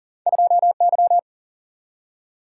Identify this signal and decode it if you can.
Morse code, heard as a keyed tone, about 42 wpm, 700 Hz. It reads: 2Y
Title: Single-Letter Single-Number 40wpm